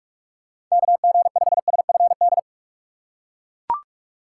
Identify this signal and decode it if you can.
Morse code, heard as a keyed tone, about 41 wpm, 700 Hz. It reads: KG5SFD
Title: USA Call Signs - 6 Characters - 3x Character Spacing - 45wpm